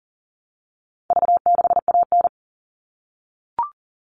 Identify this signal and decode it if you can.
Morse code, heard as a keyed tone, about 40 wpm, 700 Hz. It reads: V6AN
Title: All Cut Numbers - 4-Digit 40wpm